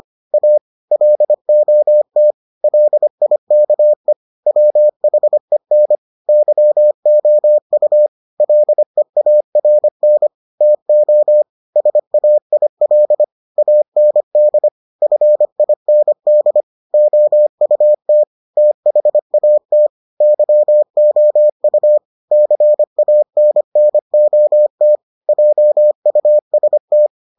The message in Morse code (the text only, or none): A LOT LIKE WHEN YOU LEARN TO SAIL AND FIND OUT THAT YOU CANNOT JUST